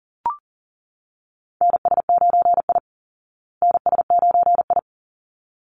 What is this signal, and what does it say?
Morse code, heard as a keyed tone, about 40 wpm, 700 Hz. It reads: NS9I NS9I
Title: Call Signs Repeated 3x - 40wpm